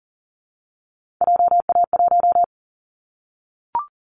Morse code, heard as a keyed tone, about 40 wpm, 700 Hz. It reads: JA1
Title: Callsign Prefixes for Top 50 DXCC - 40wpm